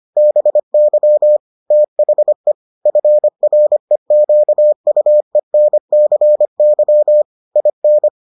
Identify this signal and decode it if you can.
Morse code, heard as a keyed tone, about 25 wpm, 600 Hz. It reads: BY THE FREQUENCY IN